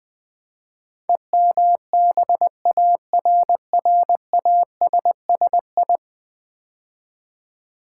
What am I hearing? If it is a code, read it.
Morse code, heard as a keyed tone, about 20 wpm, 700 Hz. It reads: EMBARRASSI
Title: Build A Word - 5 Letters At A Time - 12-Letter Words - 20wpm